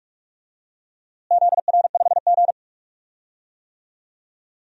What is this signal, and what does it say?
Morse code, heard as a keyed tone, about 42 wpm, 700 Hz. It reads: ZR5G
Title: DXCC-100 Call Signs - 4 Characters - Repeated 2x - 45wpm